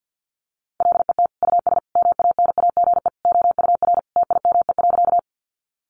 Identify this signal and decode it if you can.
Morse code, heard as a keyed tone, about 50 wpm, 700 Hz. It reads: LET US MANAGE OUR TIME.
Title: Sentences from Top 1400 Words 50wpm